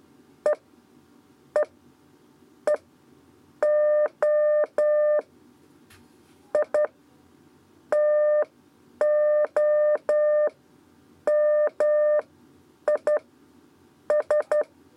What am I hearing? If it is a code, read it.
Morse code, heard as a keyed tone, about 11 wpm, spaced as for 8 wpm, 600 Hz. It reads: E E E O I TO MI S
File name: Lezione trasmessa 2 lez 2 exe